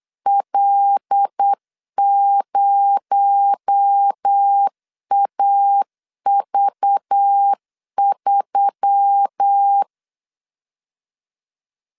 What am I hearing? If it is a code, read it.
Morse code, heard as a keyed tone, about 8 wpm, 800 Hz. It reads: L0AV3
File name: LEZIONI / 8 YJ45 / 8 YJ 8wpm 3313